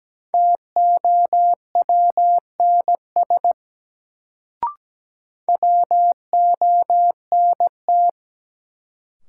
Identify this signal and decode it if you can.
Morse code, heard as a keyed tone, about 17 wpm, 700 Hz. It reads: TOWNS WONT
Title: Sentences from Top 1000 Words - Review Part 6 - New Word Drill - 17wpm